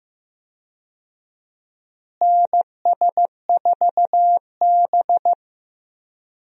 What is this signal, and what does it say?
Morse code, heard as a keyed tone, about 15 wpm, 700 Hz. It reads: NS4B
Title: USA Call Signs - 4 Characters - Repeated 3x - 15wpm